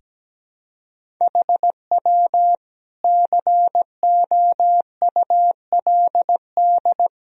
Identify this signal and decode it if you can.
Morse code, heard as a keyed tone, about 17 wpm, 700 Hz. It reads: HW COULD
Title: Sentences from Top 200 Words and Abbreviations 17wpm